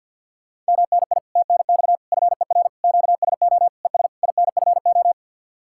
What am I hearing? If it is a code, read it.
Morse code, heard as a keyed tone, about 64 wpm, 700 Hz. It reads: MNI TNX FER QSO ES INFO